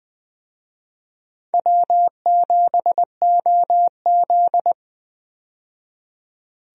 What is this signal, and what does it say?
Morse code, heard as a keyed tone, about 20 wpm, 700 Hz. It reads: W7OZ
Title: USA Call Signs - 4 Characters - 20wpm